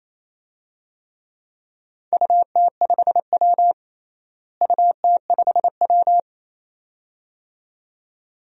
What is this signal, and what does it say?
Morse code, heard as a keyed tone, about 27 wpm, 700 Hz. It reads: UT5W UT5W
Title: DXCC-100 Call Signs - 4 Characters - Repeated 2x - 28wpm